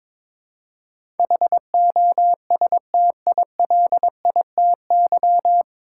Morse code, heard as a keyed tone, about 22 wpm, 700 Hz. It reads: HOSTILITY
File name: 9 Letter Words - First 3 Letters Given - 22wpm